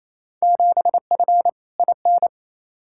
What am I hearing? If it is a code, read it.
Morse code, heard as a keyed tone, about 28 wpm, 700 Hz. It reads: 7F IN
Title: Field Day Exchanges Encore - 28wpm